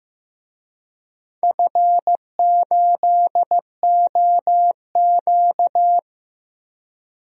Morse code, heard as a keyed tone, about 15 wpm, 700 Hz. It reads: F8OQ
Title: DXCC-100 Call Signs - 4 Characters - Repeated 3x - 15wpm